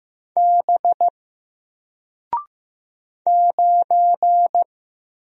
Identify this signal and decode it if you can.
Morse code, heard as a keyed tone, about 15 wpm, 700 Hz. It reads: B 9
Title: Single Letter-Number Alt 15wpm